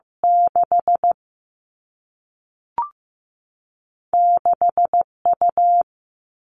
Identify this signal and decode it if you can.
Morse code, heard as a keyed tone, about 15 wpm, 700 Hz. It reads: 6 6U
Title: All Cut Numbers - 2-Digit 15wpm